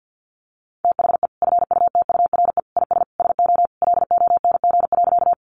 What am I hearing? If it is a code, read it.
Morse code, heard as a keyed tone, about 50 wpm, 700 Hz. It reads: THE FUTURE IS SO LONG.
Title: Sentences from Top 700 Words 50wpm